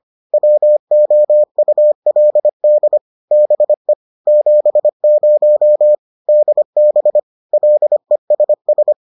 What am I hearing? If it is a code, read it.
Morse code, heard as a keyed tone, about 25 wpm, 600 Hz. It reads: WOULD BE 70 DB LESS